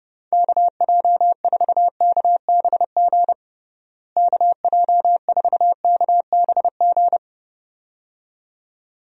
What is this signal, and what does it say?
Morse code, heard as a keyed tone, about 30 wpm, 700 Hz. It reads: KJ4KBG KJ4KBG
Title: USA Call Signs - 6 Characters - Repeated 2x - 30wpm